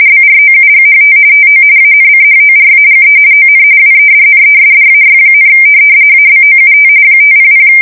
RTTY, about 45.45 baud. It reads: BROUGHT TO ANCHORAGE ON WEDNESDAY, ABOUT 500
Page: 170 Shift 2125/2295